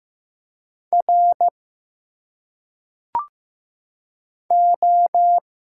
Morse code, heard as a keyed tone, about 15 wpm, 700 Hz. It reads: R O
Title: Single Letters - Phonetic Pronunciation 15wpm